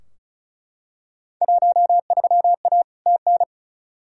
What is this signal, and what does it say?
Morse code, heard as a keyed tone, about 36 wpm, 700 Hz. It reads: 13A TN
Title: Field Day Exchanges Encore - 35wpm